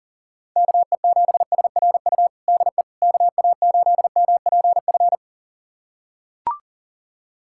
Morse code, heard as a keyed tone, about 40 wpm, 700 Hz. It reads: KE7SRU DE KA8MPF
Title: USA Callsign DE Callsign 40wpm